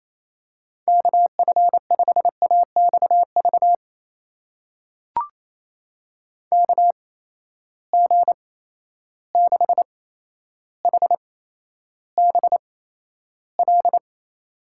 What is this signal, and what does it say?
Morse code, heard as a keyed tone, about 28 wpm, spaced as for 12 wpm, 700 Hz. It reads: KF5AXV KG6HBL
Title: USA Call Signs - 6 Characters - 8x Character Spacing - 28wpm